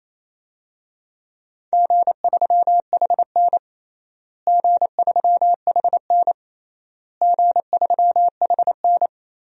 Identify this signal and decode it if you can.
Morse code, heard as a keyed tone, about 28 wpm, 700 Hz. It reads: G3HN G3HN G3HN
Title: DXCC-100 Call Signs - 4 Characters - Repeated 3x - 28wpm